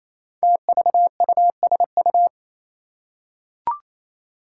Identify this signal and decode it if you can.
Morse code, heard as a keyed tone, about 27 wpm, 700 Hz. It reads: TVUSU
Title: Five Challenging Letter-Number Combinations 28wpm